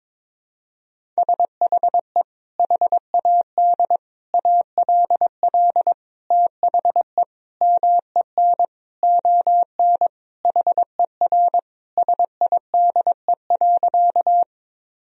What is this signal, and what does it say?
Morse code, heard as a keyed tone, about 22 wpm, 700 Hz. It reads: SHE HAD ALL THE MEN ON HER SIDE.
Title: Sentences from Top 1000 Words - Review Part 1 - 22wpm